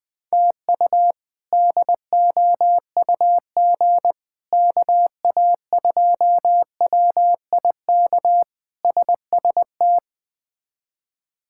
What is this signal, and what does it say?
Morse code, heard as a keyed tone, about 20 wpm, 700 Hz. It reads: TU DOUG KA2WIK SST
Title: K1USN SST QSO Practice - 20wpm